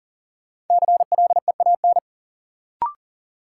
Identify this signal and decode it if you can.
Morse code, heard as a keyed tone, about 40 wpm, 700 Hz. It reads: CLEAN
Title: Sentences from Top 1100 Words - New Word Drill Alt - 40wpm